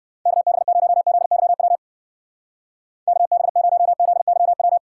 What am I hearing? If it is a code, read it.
Morse code, heard as a keyed tone, about 58 wpm, 700 Hz. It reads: KB0ZYW KB0ZYW